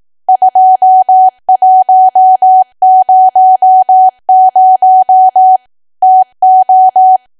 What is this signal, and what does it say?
Morse code, heard as a keyed tone, about 18 wpm, 750 Hz. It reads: 2100 TO
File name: cwa / W1AW-Level2 / 18WPM